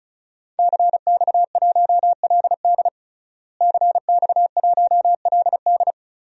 Morse code, heard as a keyed tone, about 34 wpm, 700 Hz. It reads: CX1LD CX1LD
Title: DXCC-100 Call Signs - 5 Characters - Repeated 3x - 35wpm